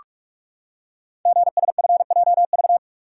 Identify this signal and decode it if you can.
Morse code, heard as a keyed tone, about 44 wpm, 690 Hz. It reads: GSFJV
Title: Code Groups - 5 Letters - 45wpm